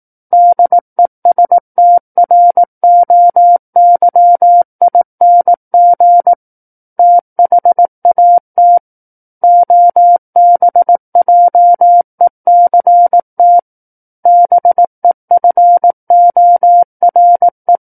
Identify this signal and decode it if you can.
Morse code, heard as a keyed tone, about 18 wpm, 700 Hz. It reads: DESTROYING THAT OBJECT BEFORE
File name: 18 WPM